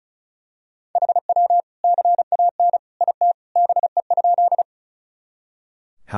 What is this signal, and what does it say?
Morse code, heard as a keyed tone, about 36 wpm, 700 Hz. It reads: HW CAN IT BE?
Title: Sentences from Top 100 Words and Abbreviations 35wpm